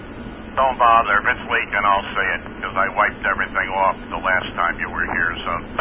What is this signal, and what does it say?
Emergency Alert System (SAME) tones, heard as harsh buzzing bursts, with speech. It is heard from 1.2 to 5.4 s.